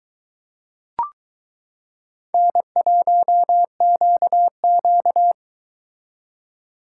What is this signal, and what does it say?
Morse code, heard as a keyed tone, about 22 wpm, 700 Hz. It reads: N1QQ